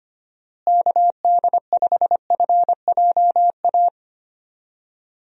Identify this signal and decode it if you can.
Morse code, heard as a keyed tone, about 25 wpm, 700 Hz. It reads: KD5FJA
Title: USA Call Signs - 6 Characters - Repeated 2x - 25wpm